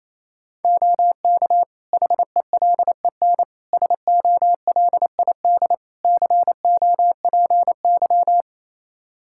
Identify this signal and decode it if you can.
Morse code, heard as a keyed tone, about 28 wpm, 700 Hz. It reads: OK HELEN SOLID COPY
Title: Practice QSO - Exchange 3 - More Info - 28wpm